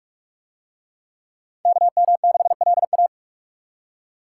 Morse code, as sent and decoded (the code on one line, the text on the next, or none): -.- -- -.... .-.. .-
KM6LA